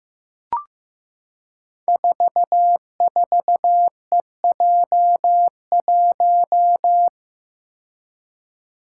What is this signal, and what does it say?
Morse code, heard as a keyed tone, about 15 wpm, 700 Hz. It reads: 44EJ1
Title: Code Groups - 5 Letter:Numbers - 15wpm